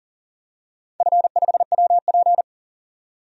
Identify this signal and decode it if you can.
Morse code, heard as a keyed tone, about 40 wpm, 700 Hz. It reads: F5WP
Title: DXCC-100 Call Signs - 4 Characters - 40wpm